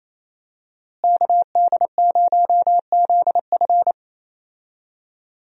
Morse code, heard as a keyed tone, about 27 wpm, 700 Hz. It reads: KD0ZF
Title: USA Call Signs - 5 Characters - Repeated 3x - 28wpm